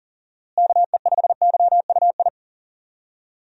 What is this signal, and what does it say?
Morse code, heard as a keyed tone, about 40 wpm, 700 Hz. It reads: KE5YUI
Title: USA Call Signs - 6 Characters - Repeated 2x - 40wpm